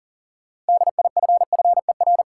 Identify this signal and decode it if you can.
Morse code, heard as a keyed tone, about 40 wpm, 700 Hz. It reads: DIFFER
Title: Sentences from Top 1000 Words - Review Part 1 - New Word Drill - 40wpm